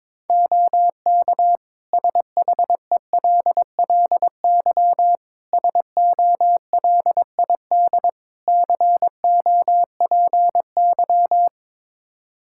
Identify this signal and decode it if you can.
Morse code, heard as a keyed tone, about 22 wpm, 700 Hz. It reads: OK SHELLY SOLID COPY
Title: Practice QSO - Exchange 3 - More Info - 22wpm